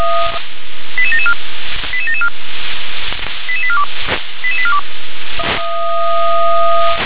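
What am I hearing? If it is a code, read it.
Five-tone selective call (ZVEI), heard as quick runs of five tones, tones of about 70 ms. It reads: 88774 88774 88842 88842
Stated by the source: ZVEI five-tone call received by SDR.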